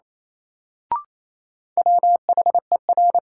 Morse code, heard as a keyed tone, about 27 wpm, 700 Hz. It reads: WHER
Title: Sentences from Top 1500 Words - New Word Drill - 28wpm